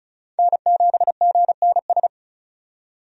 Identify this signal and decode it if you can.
Morse code, heard as a keyed tone, about 34 wpm, 700 Hz. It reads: N7GNS
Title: USA Call Signs - 5 Characters - 35wpm